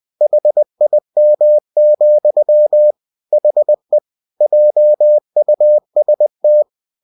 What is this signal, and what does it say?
Morse code, heard as a keyed tone, about 20 wpm, 600 Hz. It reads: HIM, HE JUST
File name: cw-books / winnie-the-pooh / winnie-20wpm-page-0021